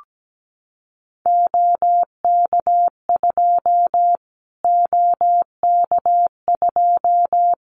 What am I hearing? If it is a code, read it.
Morse code, heard as a keyed tone, about 17 wpm, 700 Hz. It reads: OK2 OK2
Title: Callsign Prefixes for Top 25 DXCC - Repeated 3x - 17wpm